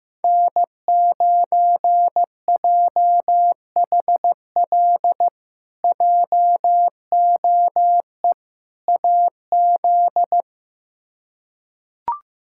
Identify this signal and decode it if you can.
Morse code, heard as a keyed tone, about 16 wpm, 700 Hz. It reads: N9JHL JOE AZ
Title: K1USN SST QSO Practice - 15wpm